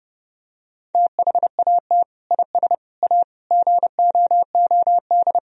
Morse code, heard as a keyed tone, about 30 wpm, 700 Hz. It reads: THAT IS A GOOD
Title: Sentences from Top 1000 Words - Review Part 3 - 30wpm